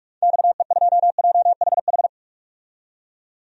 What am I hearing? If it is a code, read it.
Morse code, heard as a keyed tone, about 45 wpm, 700 Hz. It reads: XE2JHH
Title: DXCC-100 Call Signs - 6 Characters - 45wpm